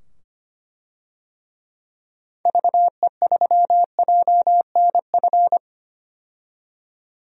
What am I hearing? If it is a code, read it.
Morse code, heard as a keyed tone, about 25 wpm, 700 Hz. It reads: VE3JNF
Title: DXCC-100 Call Signs - 6 Characters - 25wpm